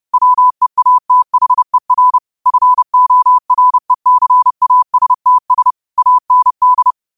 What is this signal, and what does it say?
Morse code, heard as a keyed tone, about 30 wpm, 1000 Hz. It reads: WEATHER FORECASTS AND